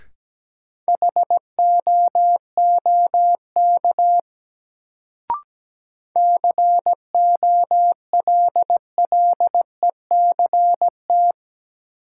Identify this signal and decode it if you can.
Morse code, heard as a keyed tone, about 17 wpm, 700 Hz. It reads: HOOK COLLECT
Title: Sentences from Top 1600 Words - New Word Drill Alt - 17wpm